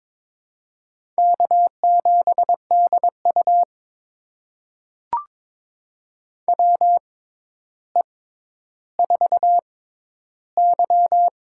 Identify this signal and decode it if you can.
Morse code, heard as a keyed tone, about 22 wpm, spaced as for 11 wpm, 700 Hz. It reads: K7DU WE4Y
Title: USA Call Signs - 6x Character Spacing - 22wpm